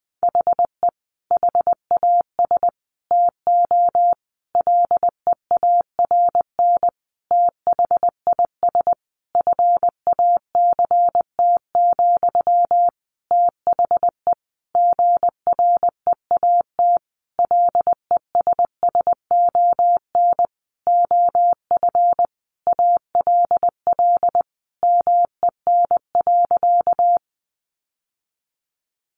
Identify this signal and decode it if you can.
Morse code, heard as a keyed tone, about 20 wpm, 700 Hz. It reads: HE HAS TO LEARN THIS FACT, THE GREAT LESSON OF ALL MEN.
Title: Sentences from Top 1200 Words 20wpm